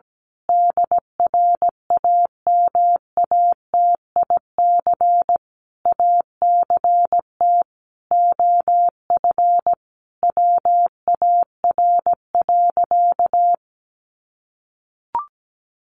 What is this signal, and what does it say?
Morse code, heard as a keyed tone, about 17 wpm, 700 Hz. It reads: DRAMATIC ACT OF WAR.
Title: Sentences from Top 1300 Words 17wpm